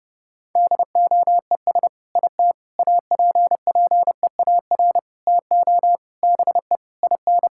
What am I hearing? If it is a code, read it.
Morse code, heard as a keyed tone, about 30 wpm, 700 Hz. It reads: DOES IT APPEAR TO BE IN
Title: Sentences from Top 1000 Words - Review Part 6 - 30wpm